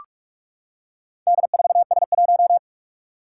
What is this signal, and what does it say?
Morse code, heard as a keyed tone, about 46 wpm, 700 Hz. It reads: D4S1
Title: Four Challenging Letter-Number Combinations 45wpm